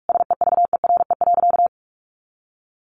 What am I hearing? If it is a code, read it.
Morse code, heard as a keyed tone, about 44 wpm, 700 Hz. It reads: SEVERE.
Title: Sentences from Top 1400 Words 45wpm